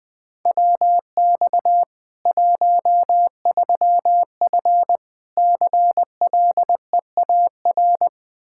Morse code, heard as a keyed tone, about 21 wpm, 700 Hz. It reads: WX 13F CLEAR
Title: K1USN SST QSO Practice 20wpm